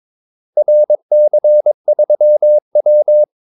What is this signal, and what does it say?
Morse code, heard as a keyed tone, about 22 wpm, 600 Hz. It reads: RC3W
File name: Callsigns-22wpm